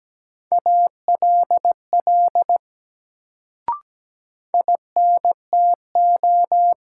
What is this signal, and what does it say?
Morse code, heard as a keyed tone, about 17 wpm, 700 Hz. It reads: ALL INTO
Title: Sets of 2 Words - New Word Drill - 17wpm